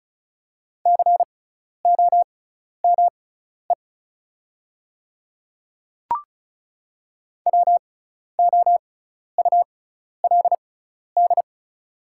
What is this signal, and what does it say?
Morse code, heard as a keyed tone, about 36 wpm, spaced as for 12 wpm, 690 Hz. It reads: COME WOULD
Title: 6x Character Spacing - Top 100 Words - 35wpm